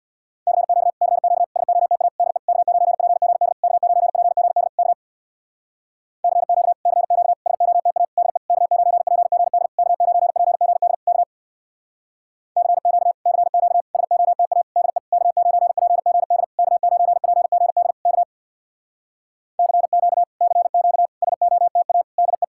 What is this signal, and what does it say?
Morse code, heard as a keyed tone, about 64 wpm, 700 Hz. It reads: CQ CQ SOTA DE K9PGD K9PGD K CQ CQ SOTA DE K9PGD K9PGD K CQ CQ SOTA DE K9PGD K9PGD K CQ CQ SOTA DE